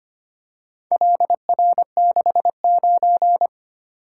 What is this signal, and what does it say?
Morse code, heard as a keyed tone, about 25 wpm, 700 Hz. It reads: LR69
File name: practice / 25wpm / Four Letter-Number Combinations - 25wpm